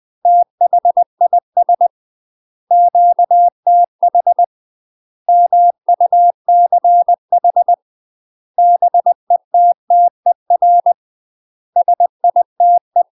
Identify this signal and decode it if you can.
Morse code, heard as a keyed tone, about 20 wpm, 700 Hz. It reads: THIS QTH MUCH BETTER SITE
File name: UK MASTER QSOs TRAINING / 20WPM / UK MASTER QSOs 0073